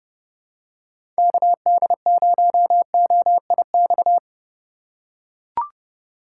Morse code, heard as a keyed tone, about 30 wpm, 700 Hz. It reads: KD0OIX
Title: K1USN SST QSO Practice - 30wpm